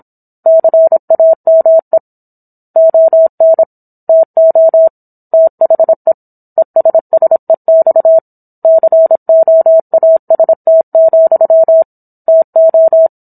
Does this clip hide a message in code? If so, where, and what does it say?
Yes: Morse code from 0.5 to 13.1 s, heard as a keyed tone, about 27 wpm, 650 Hz. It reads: CAME ON TO THE ESSEX COAST, TO